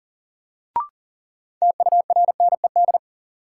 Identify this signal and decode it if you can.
Morse code, heard as a keyed tone, about 40 wpm, 700 Hz. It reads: TURNED
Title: Sets of 4 Words - New Word Drill - 40wpm